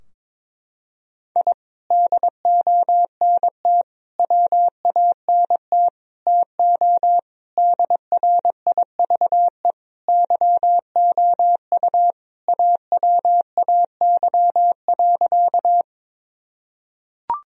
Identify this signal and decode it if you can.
Morse code, heard as a keyed tone, about 22 wpm, 700 Hz. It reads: I DONT WANT TO DRIVE YOU AWAY.
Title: Sentences from Top 1000 Words - Review Part 2 - 22wpm